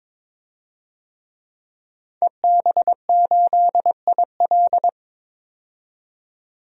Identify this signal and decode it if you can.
Morse code, heard as a keyed tone, about 22 wpm, 700 Hz. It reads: EB8IL
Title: DXCC-100 Call Signs - 5 Characters - 22wpm